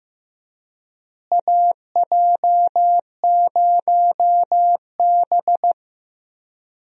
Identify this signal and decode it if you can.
Morse code, heard as a keyed tone, about 15 wpm, 700 Hz. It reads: AJ0B
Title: USA Call Signs - 4 Characters - Repeated 3x - 15wpm